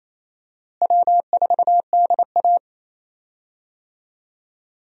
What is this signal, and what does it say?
Morse code, heard as a keyed tone, about 28 wpm, 700 Hz. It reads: W4DA
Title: USA Call Signs - 4 Characters - Repeated 2x - 28wpm